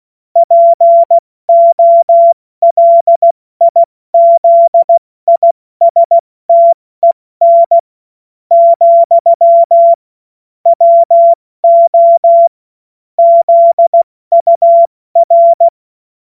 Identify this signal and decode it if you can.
Morse code, heard as a keyed tone, about 16 wpm, spaced as for 13 wpm, 680 Hz. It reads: POLIZISTEN, WO ZUR
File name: Morsekurs / Witze / Witz 17. Mai 2022 70 F80